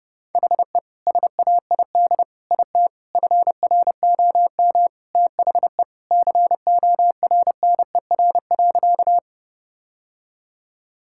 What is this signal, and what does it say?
Morse code, heard as a keyed tone, about 30 wpm, 700 Hz. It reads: HE SAID IT FROM THE CORNER.
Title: Sentences from Top 1600 Words 30wpm